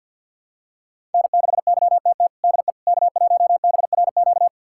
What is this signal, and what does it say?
Morse code, heard as a keyed tone, about 52 wpm, 700 Hz. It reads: N6YTT DE K1BRQ
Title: USA Callsign DE Callsign 50wpm